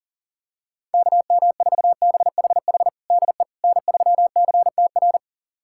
Morse code, heard as a keyed tone, about 40 wpm, 700 Hz. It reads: KM4BHH DE N3CTR
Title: USA Callsign DE Callsign 40wpm